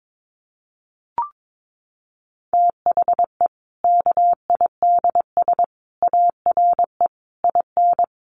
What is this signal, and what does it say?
Morse code, heard as a keyed tone, about 22 wpm, 700 Hz. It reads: THE KIDS ARE IN
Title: Easy Phrases 22wpm